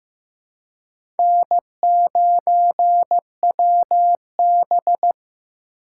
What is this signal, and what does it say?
Morse code, heard as a keyed tone, about 15 wpm, 700 Hz. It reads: N9WB
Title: USA Call Signs - 4 Characters - 3x Character Spacing - 15wpm